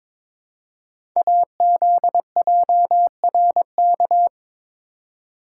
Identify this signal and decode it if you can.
Morse code, heard as a keyed tone, about 23 wpm, 700 Hz. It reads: AZJRK
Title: Code Groups - 5 Letter:Numbers - 22wpm